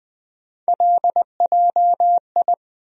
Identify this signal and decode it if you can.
Morse code, heard as a keyed tone, about 21 wpm, 700 Hz. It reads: LJI
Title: USA Call Sign Postfixes - 3 Letters 20wpm